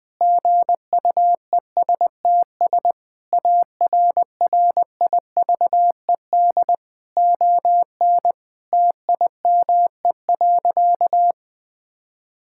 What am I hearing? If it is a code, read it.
Morse code, heard as a keyed tone, about 20 wpm, 700 Hz. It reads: GUESTS ARRIVED ON TIME.
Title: Sentences with 6-Word Phrases - 20wpm